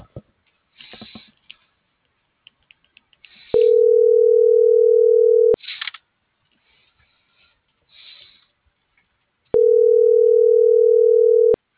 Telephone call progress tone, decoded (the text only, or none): ringback tone